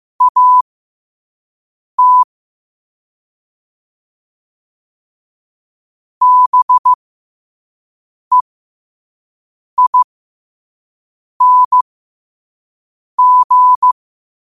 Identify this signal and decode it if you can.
Morse code, heard as a keyed tone, about 15 wpm, spaced as for 5 wpm, 1000 Hz. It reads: AT BEING
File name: 5wpm